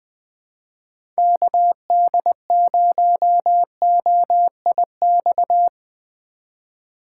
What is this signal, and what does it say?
Morse code, heard as a keyed tone, about 20 wpm, 700 Hz. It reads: KD0OIX
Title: K1USN SST QSO Practice - 20wpm